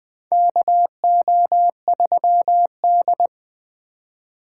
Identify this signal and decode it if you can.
Morse code, heard as a keyed tone, about 20 wpm, 700 Hz. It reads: KO3D
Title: USA Call Signs - 3x Character Spacing - 20wpm